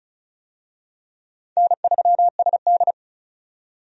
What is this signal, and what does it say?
Morse code, heard as a keyed tone, about 34 wpm, 700 Hz. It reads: N3SD
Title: USA Call Signs - 4 Characters - 35wpm